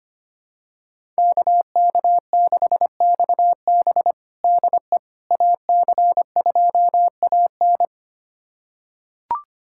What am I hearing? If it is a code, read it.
Morse code, heard as a keyed tone, about 25 wpm, 700 Hz. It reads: KK6XB DE AC2AN
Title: USA Callsign DE Callsign - 5 Characters - 25wpm